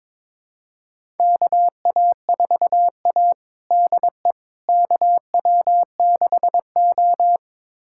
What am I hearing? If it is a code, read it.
Morse code, heard as a keyed tone, about 21 wpm, 700 Hz. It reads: KA4A DE KW6O
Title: USA Callsign DE Callsign - 4 Characters - 22wpm